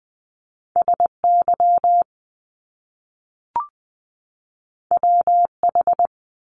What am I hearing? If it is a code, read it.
Morse code, heard as a keyed tone, about 20 wpm, 700 Hz. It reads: SY WH
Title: Top Two Letter Combinations in Words 20wpm